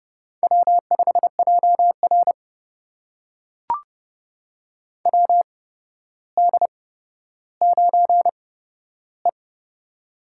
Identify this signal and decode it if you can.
Morse code, heard as a keyed tone, about 30 wpm, spaced as for 13 wpm, 700 Hz. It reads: W5JR WD9E
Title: USA Call Signs - 4 Characters - 8x Character Spacing - 30wpm